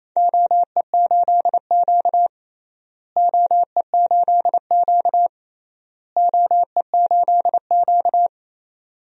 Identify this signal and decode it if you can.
Morse code, heard as a keyed tone, about 28 wpm, 700 Hz. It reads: OE8Q OE8Q OE8Q
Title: DXCC-100 Call Signs - 4 Characters - Repeated 3x - 28wpm